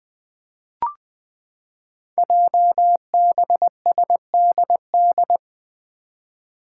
Morse code, heard as a keyed tone, about 21 wpm, 700 Hz. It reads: JBSDD
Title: Five Challenging Letter-Number Combinations 20wpm